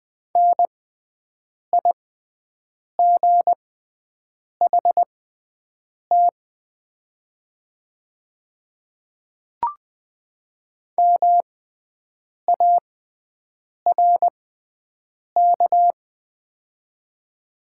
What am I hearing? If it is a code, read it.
Morse code, heard as a keyed tone, about 21 wpm, spaced as for 7 wpm, 700 Hz. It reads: NIGHT MARK
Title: 6x Character Spacing - Top 300 Words - 20wpm